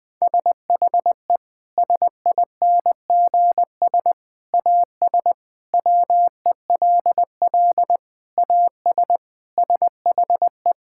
Morse code, heard as a keyed tone, about 20 wpm, 700 Hz. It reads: SHE SINGS AS WELL AS SHE
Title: Sentences with 3-Word Phrases - Spoken Beginning - 20wpm